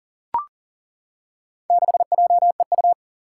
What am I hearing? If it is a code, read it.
Morse code, heard as a keyed tone, about 38 wpm, 700 Hz. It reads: 6JEU
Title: Four Challenging Letter-Number Combinations 40wpm